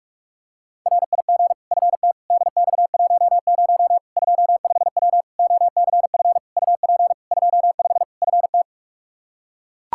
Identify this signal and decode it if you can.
Morse code, heard as a keyed tone, about 42 wpm, 700 Hz. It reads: RIG FT DX10 25W OCF UP 25 FT
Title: Practice QSO - Exchange 2 - Info - 45wpm